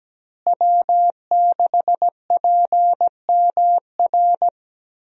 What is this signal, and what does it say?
Morse code, heard as a keyed tone, about 17 wpm, 700 Hz. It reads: W6PMR
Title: USA Call Signs - 5 Characters - Repeated 3x - 17wpm